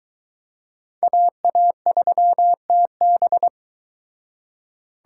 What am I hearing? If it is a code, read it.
Morse code, heard as a keyed tone, about 22 wpm, 700 Hz. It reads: AA3TB